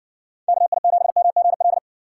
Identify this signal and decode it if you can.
Morse code, heard as a keyed tone, about 56 wpm, 700 Hz. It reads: KI7MGL